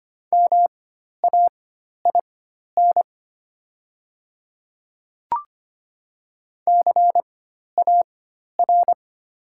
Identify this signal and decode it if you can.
Morse code, heard as a keyed tone, about 25 wpm, spaced as for 12 wpm, 700 Hz. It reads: MAIN CAR
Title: 4x Character Spacing - Top 300 Words - 25wpm